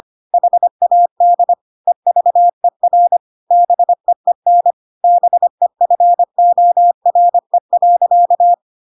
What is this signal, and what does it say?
Morse code, heard as a keyed tone, about 25 wpm, 700 Hz. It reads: HAD EVER BEEN BEFORE.
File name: Pinocchio / Pinocchio ch3 25 25 ik1hge